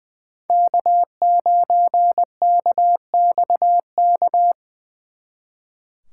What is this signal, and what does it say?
Morse code, heard as a keyed tone, about 23 wpm, 700 Hz. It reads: K9KXK
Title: Speed Racing - USA 1x3 Call Signs 20wpm